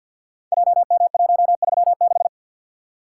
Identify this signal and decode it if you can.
Morse code, heard as a keyed tone, about 47 wpm, 700 Hz. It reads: JM136
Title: Code Groups - 5 Letter:Numbers - 50wpm